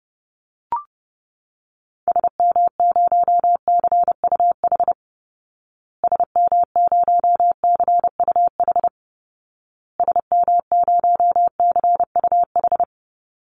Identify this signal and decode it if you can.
Morse code, heard as a keyed tone, about 30 wpm, 700 Hz. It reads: SM0CUH SM0CUH SM0CUH
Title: Call Signs Repeated 2x - 30wpm